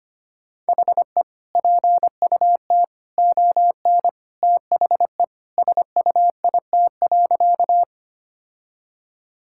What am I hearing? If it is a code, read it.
Morse code, heard as a keyed tone, about 25 wpm, 700 Hz. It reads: HE PUT ON THE SUIT.
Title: Sentences from Top 1600 Words 25wpm